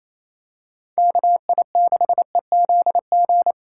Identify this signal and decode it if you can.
Morse code, heard as a keyed tone, about 28 wpm, 700 Hz. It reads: KI6EZG
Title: USA Call Signs - 2x Character Spacing - 28wpm